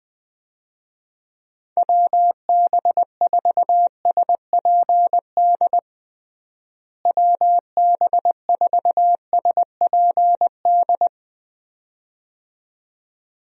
Morse code, heard as a keyed tone, about 20 wpm, 700 Hz. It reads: WB4SPD WB4SPD
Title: USA Call Signs - 6 Characters - Repeated 2x - 20wpm